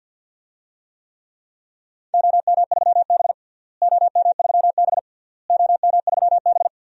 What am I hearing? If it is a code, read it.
Morse code, heard as a keyed tone, about 50 wpm, 700 Hz. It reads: OM3B OM3B OM3B
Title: DXCC-100 Call Signs - 4 Characters - Repeated 3x - 50wpm